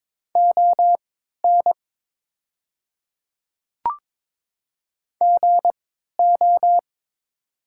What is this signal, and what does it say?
Morse code, heard as a keyed tone, about 22 wpm, spaced as for 12 wpm, 700 Hz. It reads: ON GO
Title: 3x Character Spacing - Top 300 Words - 22wpm